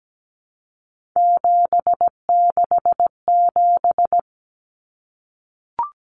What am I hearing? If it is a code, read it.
Morse code, heard as a keyed tone, about 17 wpm, 700 Hz. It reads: 767
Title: Numbers - 3-Digit 17wpm